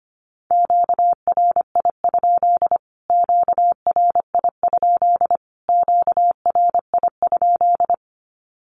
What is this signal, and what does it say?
Morse code, heard as a keyed tone, about 25 wpm, 700 Hz. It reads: QRI? QRI? QRI?
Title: Q-Codes Repeated 3x 25wpm